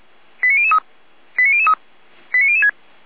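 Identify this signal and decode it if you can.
Five-tone selective call (ZVEI), heard as quick runs of five tones, tones of about 70 ms. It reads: 89003 89003 89006
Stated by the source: Five-tone ZVEI selective call received by SDR, fire brigade Orlamünde (feuerwehr-orlamuende).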